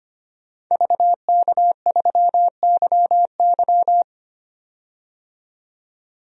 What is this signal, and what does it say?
Morse code, heard as a keyed tone, about 25 wpm, 700 Hz. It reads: VK3YY
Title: DXCC-100 Call Signs - 5 Characters - 25wpm